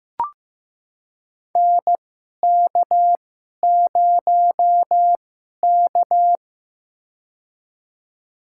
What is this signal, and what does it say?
Morse code, heard as a keyed tone, about 15 wpm, spaced as for 11 wpm, 700 Hz. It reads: NK0K
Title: USA Call Signs - 4 Characters - 2x Character Spacing - 15wpm